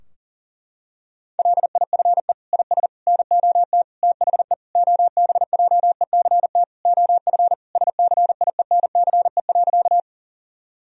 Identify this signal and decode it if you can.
Morse code, heard as a keyed tone, about 40 wpm, 700 Hz. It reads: LIFE IS NOT THE OBJECT OF SCIENCE.
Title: Sentences from Top 1000 Words - Review Part 3 - 40wpm